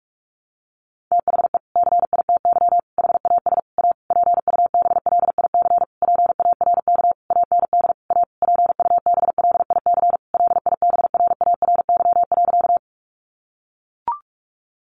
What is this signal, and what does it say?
Morse code, heard as a keyed tone, about 44 wpm, 700 Hz. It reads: THE CITY HAS A PUBLIC PARK AND A PUBLIC LIBRARY.
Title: Sentences from Top 1300 Words 45wpm